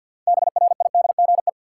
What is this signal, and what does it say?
Morse code, heard as a keyed tone, about 53 wpm, 700 Hz. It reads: BRIDGE
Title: Sentences from Top 1200 Words - New Word Drill - 50wpm